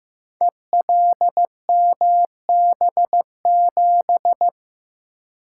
Morse code, heard as a keyed tone, about 15 wpm, 700 Hz. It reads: ELMB7
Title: Code Groups - 5 Letter:Numbers - 15wpm